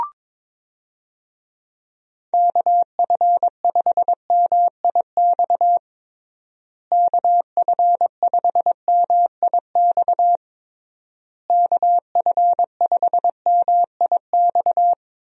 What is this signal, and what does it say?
Morse code, heard as a keyed tone, about 22 wpm, 700 Hz. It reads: KF5MIX KF5MIX KF5MIX
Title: USA Call Signs - 6 Characters - Repeated 3x - 22wpm